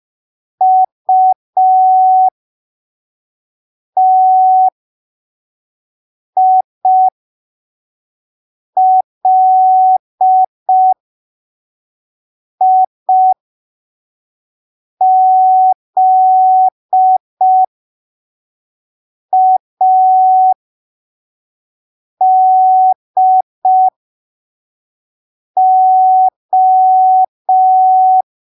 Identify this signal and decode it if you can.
Morse code, heard as a keyed tone, about 5 wpm, spaced as for 3 wpm, 750 Hz. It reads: UTILIZADO
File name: arquivos / texto / texto 10 05ppm